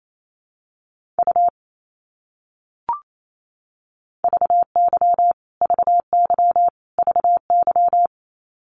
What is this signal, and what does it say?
Morse code, heard as a keyed tone, about 28 wpm, 700 Hz. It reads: U VY VY VY
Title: Abbreviations Repeated 3x - 28wpm